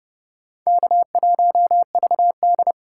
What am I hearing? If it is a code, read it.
Morse code, heard as a keyed tone, about 32 wpm, 700 Hz. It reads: K1VD